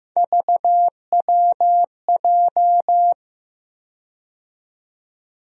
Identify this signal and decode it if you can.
Morse code, heard as a keyed tone, about 15 wpm, 700 Hz. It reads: VWJ
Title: USA Call Sign Postfixes - 3 Letters 15wpm